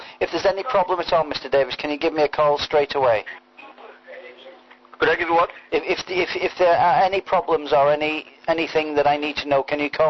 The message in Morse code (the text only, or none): none